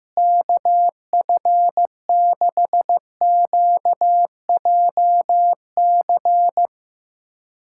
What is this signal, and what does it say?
Morse code, heard as a keyed tone, about 15 wpm, 700 Hz. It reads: KF6QJC
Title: USA Callsign DE Callsign 15wpm